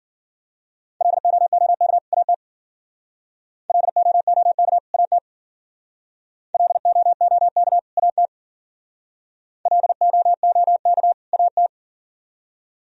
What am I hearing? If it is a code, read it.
Morse code, heard as a keyed tone, about 50 wpm, 700 Hz. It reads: LOOK AT LOOK AT LOOK AT LOOK AT
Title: Speed-Racing - Sets of 2 Words 40wpm